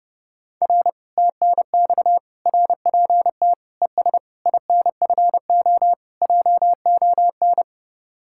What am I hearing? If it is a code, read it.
Morse code, heard as a keyed tone, about 30 wpm, 700 Hz. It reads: R TNX RPT ES INFO JON